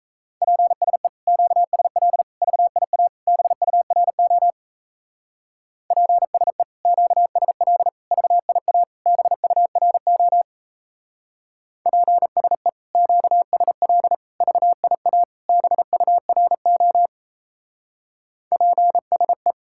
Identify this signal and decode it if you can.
Morse code, heard as a keyed tone, about 35 wpm, 700 Hz. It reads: PSE QSL VIA BURO PSE QSL VIA BURO PSE QSL VIA BURO PSE